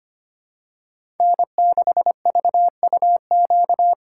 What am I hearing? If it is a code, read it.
Morse code, heard as a keyed tone, about 25 wpm, 700 Hz. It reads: N6VUQ
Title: USA Call Signs - 5 Characters - 2x Character Spacing - 25wpm